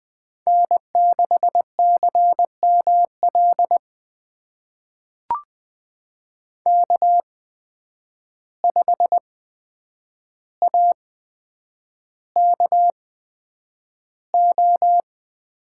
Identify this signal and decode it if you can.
Morse code, heard as a keyed tone, about 20 wpm, spaced as for 9 wpm, 700 Hz. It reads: N6CML K5AKO
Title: USA Call Signs - 8x Character Spacing - 20wpm